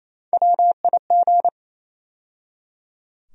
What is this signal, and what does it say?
Morse code, heard as a keyed tone, about 28 wpm, 700 Hz. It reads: WIG
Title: USA Call Sign Postfixes - 3 Letters 28wpm